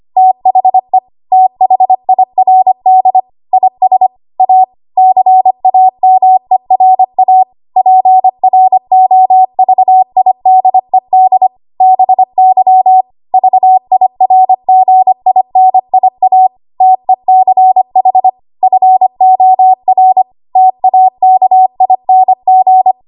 Morse code, heard as a keyed tone, about 25 wpm, 750 Hz. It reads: THE THIRD IS A CAMERA PROVIDED BY VIRGINIA TECH FOR TAKING
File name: Archive / 25 WPM / 25WPM